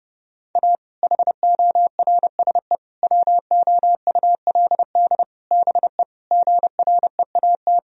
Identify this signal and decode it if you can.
Morse code, heard as a keyed tone, about 32 wpm, 700 Hz. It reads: A HORSE WOULD BE GREAT
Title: Sentences from Top 1000 Words - Review Part 1 - 30wpm